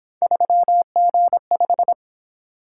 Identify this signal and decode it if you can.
Morse code, heard as a keyed tone, about 28 wpm, 700 Hz. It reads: 3G5